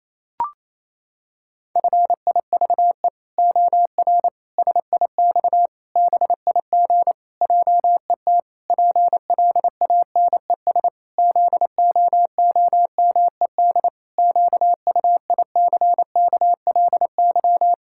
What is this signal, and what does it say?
Morse code, heard as a keyed tone, about 28 wpm, 700 Hz. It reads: FIVE OR SIX BIG JET PLANES ZOOMED QUICKLY
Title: Pangram Challenge v2 - 28wpm